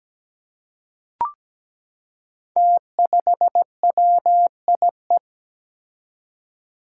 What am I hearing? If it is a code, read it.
Morse code, heard as a keyed tone, about 17 wpm, 700 Hz. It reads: T5WIE
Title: Five Challenging Letter-Number Combinations 17wpm